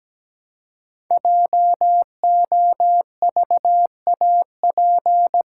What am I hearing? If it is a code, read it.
Morse code, heard as a keyed tone, about 17 wpm, 700 Hz. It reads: JOVAP
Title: Code Groups - 5 Letters - 17wpm